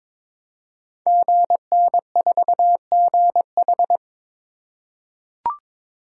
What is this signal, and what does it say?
Morse code, heard as a keyed tone, about 21 wpm, 700 Hz. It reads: GN4GH
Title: Code Groups - 5 Letter:Numbers - 22wpm